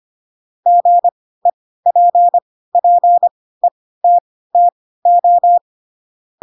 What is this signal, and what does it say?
Morse code, heard as a keyed tone, about 25 wpm, spaced as for 16 wpm, 700 Hz. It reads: GEPPETTO